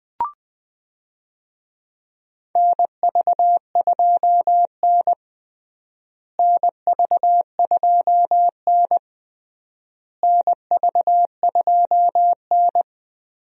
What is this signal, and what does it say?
Morse code, heard as a keyed tone, about 20 wpm, 700 Hz. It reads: NV2N NV2N NV2N
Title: USA Call Signs - 4 Characters - Repeated 3x - 20wpm